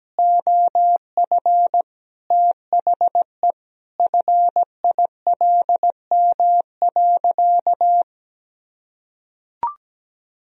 Morse code, heard as a keyed tone, about 17 wpm, 700 Hz. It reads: OF THE FILM.
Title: Sentences from Top 1000 Words - Review Part 3 - 17wpm